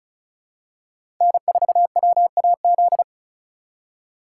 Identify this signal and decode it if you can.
Morse code, heard as a keyed tone, about 34 wpm, 700 Hz. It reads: N4WAZ
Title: USA Call Signs - 5 Characters - 4x Character Spacing - 35wpm